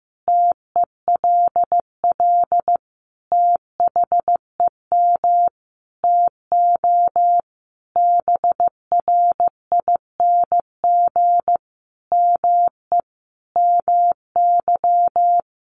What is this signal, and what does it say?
Morse code, heard as a keyed tone, about 14 wpm, 700 Hz. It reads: TELL THEM TO BRING ME MY
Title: Sentences from Top 500 Words 15wpm